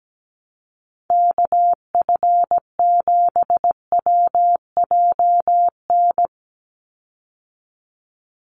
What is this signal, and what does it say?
Morse code, heard as a keyed tone, about 17 wpm, 700 Hz. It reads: KF7WJN
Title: USA Call Signs 17wpm